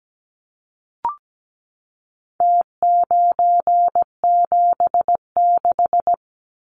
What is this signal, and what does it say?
Morse code, heard as a keyed tone, about 17 wpm, 700 Hz. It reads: T976
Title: Numbers - Contest Style - 4-Digit 17wpm